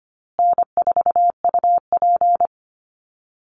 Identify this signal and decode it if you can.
Morse code, heard as a keyed tone, about 25 wpm, 700 Hz. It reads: N4UP
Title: Easy Call Signs Repeated 3x - 25wpm